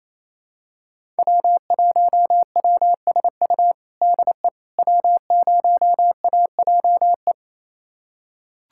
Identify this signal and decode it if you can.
Morse code, heard as a keyed tone, about 28 wpm, 700 Hz. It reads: W1WSU DE W0AJE
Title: USA Callsign DE Callsign - 5 Characters - 28wpm